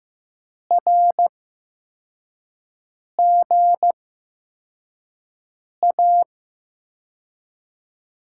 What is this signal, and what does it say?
Morse code, heard as a keyed tone, about 15 wpm, spaced as for 4 wpm, 700 Hz. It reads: RGA
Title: USA Call Signs - 8x Character Spacing - 15wpm